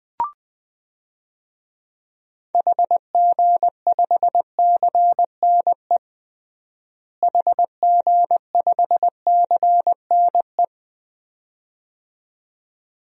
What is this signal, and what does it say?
Morse code, heard as a keyed tone, about 20 wpm, 700 Hz. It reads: HG5CNE HG5CNE
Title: DXCC-100 Call Signs - 6 Characters - Repeated 2x - 20wpm